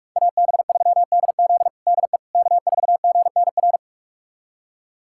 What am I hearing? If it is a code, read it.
Morse code, heard as a keyed tone, about 46 wpm, 700 Hz. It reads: AB3DZ DE K4GNR
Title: USA Callsign DE Callsign - 5 Characters - 45wpm